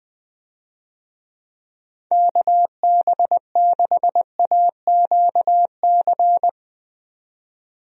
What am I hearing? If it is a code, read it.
Morse code, heard as a keyed tone, about 20 wpm, 700 Hz. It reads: KB6AQC
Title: USA Call Signs - 6 Characters - 20wpm